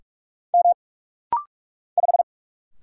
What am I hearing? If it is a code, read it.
Morse code, heard as a keyed tone, about 44 wpm, 700 Hz. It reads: M 5
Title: Single Letter-Number - Rapid-Fire 45wpm